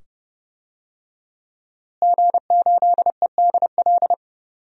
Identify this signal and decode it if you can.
Morse code, heard as a keyed tone, about 30 wpm, 700 Hz. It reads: G8EDL
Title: DXCC-100 Call Signs - 5 Characters - Repeated 2x - 30wpm